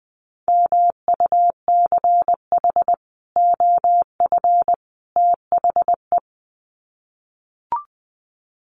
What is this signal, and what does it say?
Morse code, heard as a keyed tone, about 20 wpm, 700 Hz. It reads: MUCH OF THE
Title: Sets of 3 Words 20wpm